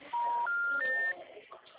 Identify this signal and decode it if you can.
Special information tone, with speech.